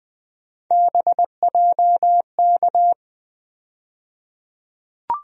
Morse code, heard as a keyed tone, about 20 wpm, 700 Hz. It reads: BJK
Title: USA Call Sign Postfixes - 3 Letters 20wpm